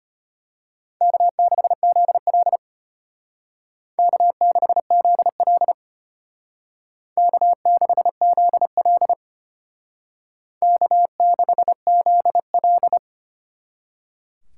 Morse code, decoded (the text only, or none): K6ZL K6ZL K6ZL K6ZL